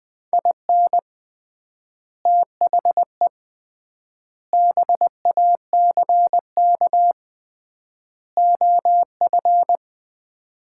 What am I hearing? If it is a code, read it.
Morse code, heard as a keyed tone, about 20 wpm, 700 Hz. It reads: IN THE BACK OF
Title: 3x Word-Spacing - Sets of 4 Words - 20wpm